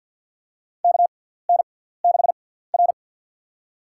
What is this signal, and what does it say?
Morse code, heard as a keyed tone, about 50 wpm, spaced as for 17 wpm, 700 Hz. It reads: KN6R
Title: USA Call Signs - 6x Character Spacing - 50wpm